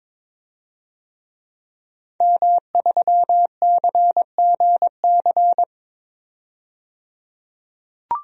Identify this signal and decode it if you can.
Morse code, heard as a keyed tone, about 22 wpm, 700 Hz. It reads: M3CGC
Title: DXCC-100 Call Signs - 5 Characters - 22wpm